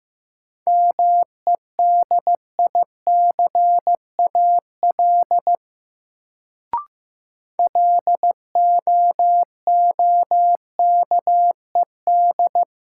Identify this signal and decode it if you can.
Morse code, heard as a keyed tone, about 15 wpm, 700 Hz. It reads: MEDICAL LOOKED
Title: Sentences from Top 800 Words - New Word Drill - 15wpm